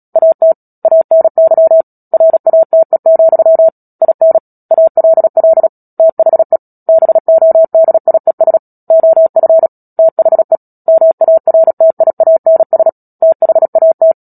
Morse code, heard as a keyed tone, about 37 wpm, 650 Hz. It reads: AT ANY RATE, IN ALL THE BODIES OF THE MARTIANS THAT
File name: Book Two / 10-Chapter 10 - Epilogue - 36 WPM